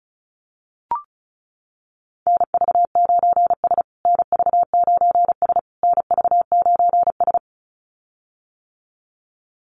Morse code, heard as a keyed tone, about 34 wpm, 700 Hz. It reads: NV9S NV9S NV9S
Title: USA Call Signs Repeated 3x 35wpm